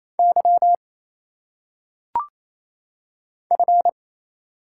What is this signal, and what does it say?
Morse code, heard as a keyed tone, about 27 wpm, 700 Hz. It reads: Y F
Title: Single Letters - Phonetic Pronunciation 28wpm